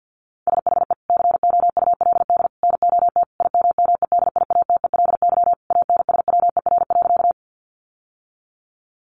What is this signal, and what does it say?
Morse code, heard as a keyed tone, about 50 wpm, 700 Hz. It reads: SHE COULD NOT IMMEDIATELY ANSWER.
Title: Sentences from Top 1100 Words 50wpm